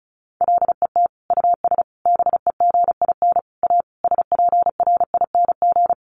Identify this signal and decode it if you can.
Morse code, heard as a keyed tone, about 34 wpm, 700 Hz. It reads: LET US BEGIN A SPRING
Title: Sentences from Top 1400 Words 35wpm